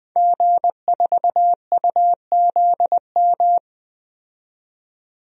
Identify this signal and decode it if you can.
Morse code, heard as a keyed tone, about 20 wpm, 700 Hz. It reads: G4UZM
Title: DXCC-100 Call Signs - 5 Characters - 20wpm